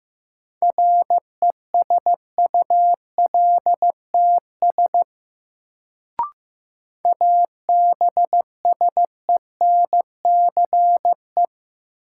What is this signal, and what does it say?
Morse code, heard as a keyed tone, about 15 wpm, 700 Hz. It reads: RESULTS ABSENCE
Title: Sets of 4 Words - New Word Drill - 15wpm